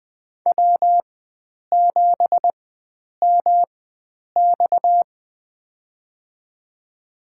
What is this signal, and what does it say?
Morse code, heard as a keyed tone, about 20 wpm, spaced as for 9 wpm, 700 Hz. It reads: W7MX
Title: USA Call Signs - 4 Characters - 4x Character Spacing - 20wpm